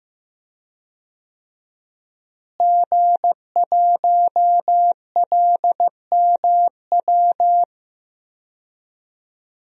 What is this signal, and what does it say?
Morse code, heard as a keyed tone, about 15 wpm, 700 Hz. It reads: G1LMW
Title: DXCC-100 Call Signs - 5 Characters - Repeated 2x - 15wpm